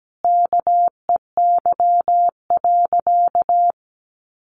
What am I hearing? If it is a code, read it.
Morse code, heard as a keyed tone, about 17 wpm, 700 Hz. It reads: KEY.
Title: Sentences from Top 1400 Words 17wpm